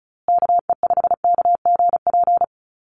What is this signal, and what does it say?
Morse code, heard as a keyed tone, about 34 wpm, 700 Hz. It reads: KE5KGP
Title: USA Call Signs Repeated 3x 35wpm